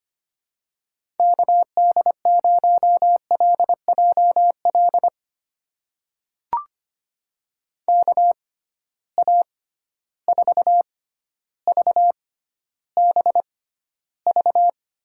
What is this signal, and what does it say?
Morse code, heard as a keyed tone, about 25 wpm, spaced as for 13 wpm, 700 Hz. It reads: KD0LJL KA4VBV
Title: USA Call Signs - 6 Characters - 6x Character Spacing - 25wpm